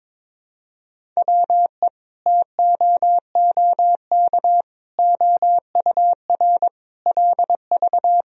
Morse code, heard as a keyed tone, about 22 wpm, 700 Hz. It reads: WE TOOK OUR LV
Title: Sentences from Top 400 Words and Abbreviations 22wpm